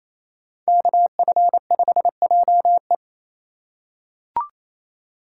Morse code, heard as a keyed tone, about 28 wpm, 700 Hz. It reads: KF5JE
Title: USA Call Signs - 5 Characters - 4x Character Spacing - 28wpm